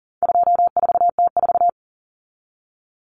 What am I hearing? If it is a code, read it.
Morse code, heard as a keyed tone, about 40 wpm, 700 Hz. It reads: 24T4
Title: Common Cut Numbers - 4-Digit 40wpm